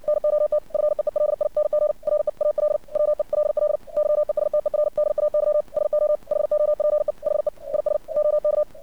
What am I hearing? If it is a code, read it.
Morse code, heard as a keyed tone, about 59 wpm, 600 Hz. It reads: NOT FEELING REAL WELL YESTERDAY SO HOPE HE IS OK
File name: archived / CRSnet